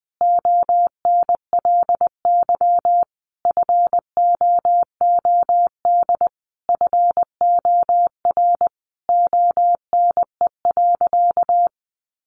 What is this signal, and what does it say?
Morse code, heard as a keyed tone, about 20 wpm, 700 Hz. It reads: ONLY FOOD FOR ONE.
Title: Sentences from Top 300 Words 20wpm